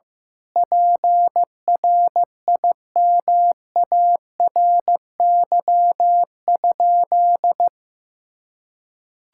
Morse code, heard as a keyed tone, about 15 wpm, 700 Hz. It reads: PRIMARY?
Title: Sentences from Top 1000 Words - Review Part 5 - 15wpm